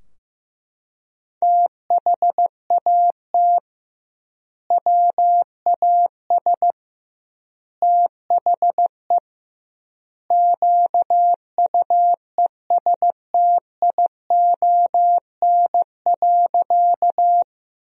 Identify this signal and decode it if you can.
Morse code, heard as a keyed tone, about 16 wpm, 700 Hz. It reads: THAT WAS THE QUESTION.
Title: 2x Word-Spacing - Sentences from Top 400 Words - 15wpm